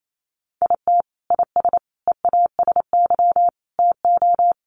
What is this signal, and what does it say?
Morse code, heard as a keyed tone, about 28 wpm, 700 Hz. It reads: IT IS EASY TO
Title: Sentences from Top 1100 Words 28wpm